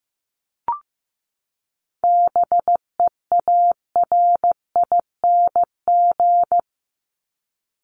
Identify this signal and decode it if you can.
Morse code, heard as a keyed tone, about 15 wpm, 700 Hz. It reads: BEARING
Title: ING Suffix - Top 100 Words with Base Words 15wpm